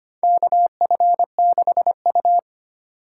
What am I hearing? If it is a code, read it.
Morse code, heard as a keyed tone, about 25 wpm, 700 Hz. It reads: KF6U
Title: USA Call Signs - 4 Characters - 4x Character Spacing - 25wpm